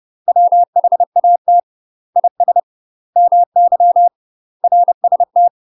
Morse code, heard as a keyed tone, about 30 wpm, 700 Hz. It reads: WHAT IS MY RST
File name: UK MASTER QSOs TRAINING / 30WPM / UK MASTER QSOs 0023